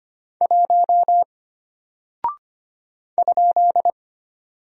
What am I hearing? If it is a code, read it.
Morse code, heard as a keyed tone, about 25 wpm, 700 Hz. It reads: 1 ?
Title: Single Character (Letter:Number:Punctuation:Prosign) Alt 25wpm